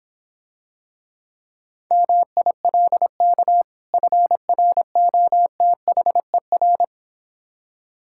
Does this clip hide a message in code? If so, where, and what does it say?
Yes: Morse code from 1.9 to 6.8 s, heard as a keyed tone, about 26 wpm, 700 Hz. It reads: MILK FROTHER